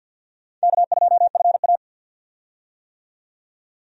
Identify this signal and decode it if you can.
Morse code, heard as a keyed tone, about 52 wpm, 700 Hz. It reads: K2FA
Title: USA Call Signs - 4 Characters - 50wpm